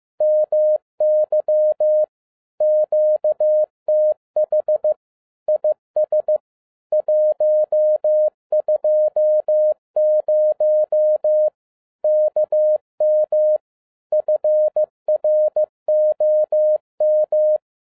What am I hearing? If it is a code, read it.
Morse code, heard as a keyed tone, about 15 wpm, 610 Hz. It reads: MY QTH IS 120 KM FROM
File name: FIELD DAY CONTEST QRV FROM SAKAE VILLAGE 15wpm